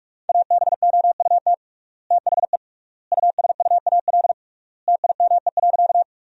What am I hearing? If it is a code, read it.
Morse code, heard as a keyed tone, about 46 wpm, 700 Hz. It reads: ABOUT THE USUAL TIME.
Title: 3x Word-Spacing - Sentences from Top 400 Words - 45wpm